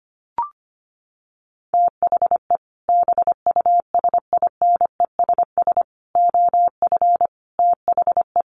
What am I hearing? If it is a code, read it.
Morse code, heard as a keyed tone, about 25 wpm, 700 Hz. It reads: THE BUSINESS OF THE
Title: Sentences from Top 1500 Words 25wpm